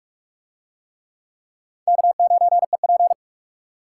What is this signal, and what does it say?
Morse code, heard as a keyed tone, about 39 wpm, 700 Hz. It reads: K9EP